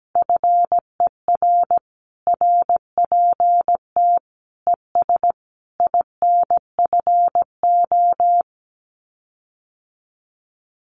Morse code, heard as a keyed tone, about 17 wpm, 700 Hz. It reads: FER RPT ES INFO
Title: Common QSO Phrases Encore 17wpm